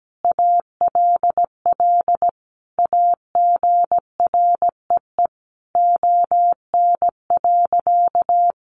Morse code, heard as a keyed tone, about 17 wpm, 700 Hz. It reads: ALL AGREE ON.
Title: Sentences from Top 1100 Words 17wpm